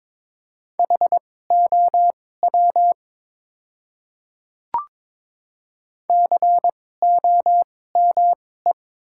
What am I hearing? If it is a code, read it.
Morse code, heard as a keyed tone, about 21 wpm, spaced as for 16 wpm, 700 Hz. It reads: HOW COME
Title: 2x Character Spacing - Top 100 Words - 22wpm